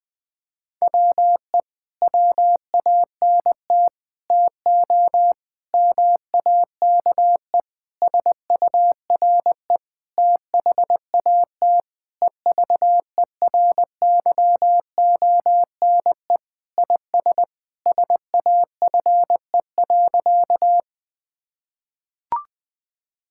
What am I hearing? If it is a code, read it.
Morse code, heard as a keyed tone, about 21 wpm, 700 Hz. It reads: WE WANT TO MAKE SURE THAT EVERYONE IS SAFE.
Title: Sentences with 5-Word Phrases - 20wpm